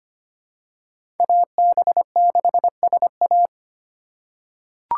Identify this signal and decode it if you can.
Morse code, heard as a keyed tone, about 25 wpm, 700 Hz. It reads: AB6SA
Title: USA Call Signs - 5 Characters - 6x Character Spacing - 25wpm